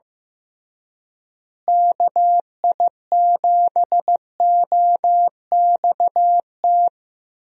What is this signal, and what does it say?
Morse code, heard as a keyed tone, about 15 wpm, 700 Hz. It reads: KI7OXT
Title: USA Call Signs - 6 Characters - Repeated 2x - 15wpm